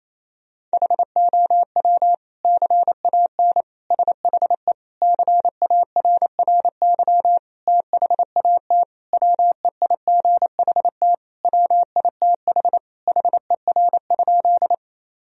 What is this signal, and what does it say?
Morse code, heard as a keyed tone, about 28 wpm, 700 Hz. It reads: HOW CAN SHE CARRY THAT WEIGHT WITH HER?
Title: Sentences from Top 1000 Words - Review Part 3 - 28wpm